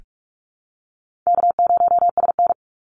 Morse code, heard as a keyed tone, about 44 wpm, 700 Hz. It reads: K0SN
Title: Call Signs Repeated 3x - 45wpm